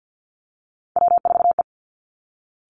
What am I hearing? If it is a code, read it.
Morse code, heard as a keyed tone, about 50 wpm, 700 Hz. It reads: W4E
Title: Three Challenging Letter-Number Combinations 50wpm